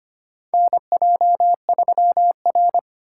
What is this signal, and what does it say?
Morse code, heard as a keyed tone, about 25 wpm, 700 Hz. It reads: NJ3R
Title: USA Call Signs - 6x Character Spacing - 25wpm